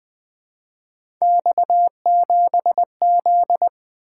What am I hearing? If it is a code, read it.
Morse code, heard as a keyed tone, about 20 wpm, 700 Hz. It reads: X7Z
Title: Three Letter-Number Combinations - 20wpm